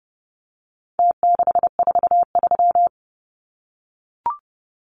Morse code, heard as a keyed tone, about 30 wpm, 700 Hz. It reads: T643
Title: Numbers - Contest Style - 4-Digit 30wpm